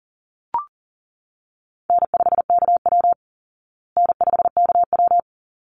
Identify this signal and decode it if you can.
Morse code, heard as a keyed tone, about 40 wpm, 700 Hz. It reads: N5KW N5KW
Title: Easy Call Signs Repeated 2x - 40wpm